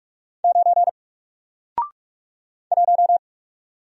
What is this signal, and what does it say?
Morse code, heard as a keyed tone, about 44 wpm, 700 Hz. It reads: 9 1
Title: Single Letter-Number Alt 45wpm